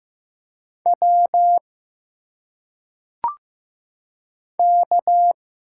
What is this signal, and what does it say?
Morse code, heard as a keyed tone, about 15 wpm, 700 Hz. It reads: W K
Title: Single Letters - Phonetic Pronunciation 15wpm